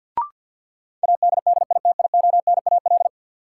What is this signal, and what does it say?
Morse code, heard as a keyed tone, about 52 wpm, 700 Hz. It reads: ADDITIONAL
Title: Sentences from Top 1400 Words - New Word Drill Alt - 50wpm